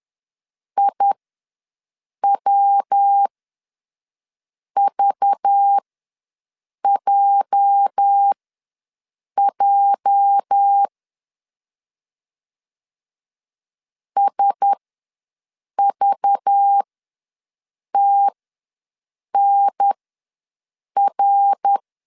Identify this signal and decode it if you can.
Morse code, heard as a keyed tone, about 10 wpm, spaced as for 6 wpm, 800 Hz. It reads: IWVJJ SVTNR
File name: LEZIONI / 8 YJ45 / 8 YJ ws 10wpm 4043